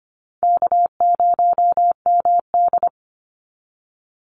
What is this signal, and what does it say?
Morse code, heard as a keyed tone, about 25 wpm, 700 Hz. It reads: K0MD
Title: Easy Call Signs - 25wpm